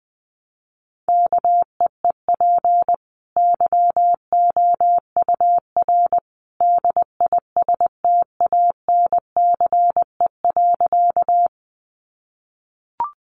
Morse code, heard as a keyed tone, about 20 wpm, 700 Hz. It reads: KEEP YOUR DISTANCE.
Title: Sentences from Top 1200 Words 20wpm